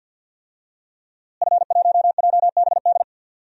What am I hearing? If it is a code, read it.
Morse code, heard as a keyed tone, about 47 wpm, 700 Hz. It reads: F1JBD
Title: DXCC-100 Call Signs - 5 Characters - Repeated 3x - 50wpm